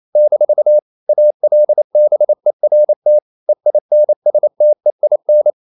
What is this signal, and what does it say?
Morse code, heard as a keyed tone, about 28 wpm, 600 Hz. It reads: - ALBERT EINSTEIN
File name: CW / Practice / EnglishQuotations0228